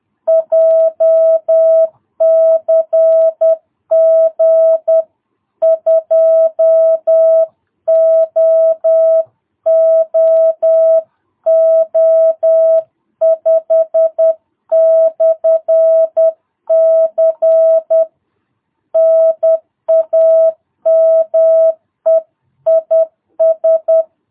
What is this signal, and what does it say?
Morse code, heard as a keyed tone, about 10 wpm, 650 Hz. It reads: JCG 2OOO5/C NAME IS
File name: jh8cbh / 2022cw